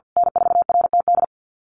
Morse code, heard as a keyed tone, about 48 wpm, 700 Hz. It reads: N4RTD
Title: USA Call Signs Repeated 3x 50wpm